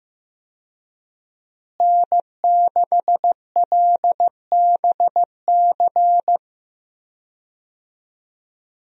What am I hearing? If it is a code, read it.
Morse code, heard as a keyed tone, about 15 wpm, 700 Hz. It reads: N6LBC
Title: USA Call Signs - 5 Characters - Repeated 3x - 15wpm